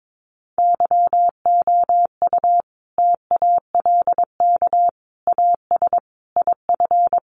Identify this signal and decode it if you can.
Morse code, heard as a keyed tone, about 22 wpm, 700 Hz. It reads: YOU TALK AS IF
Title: Sentences from Top 1100 Words 22wpm